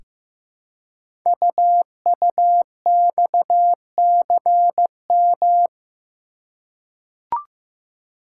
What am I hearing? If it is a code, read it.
Morse code, heard as a keyed tone, about 15 wpm, 700 Hz. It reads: UUXCM
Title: Code Groups - 5 Letters - 15wpm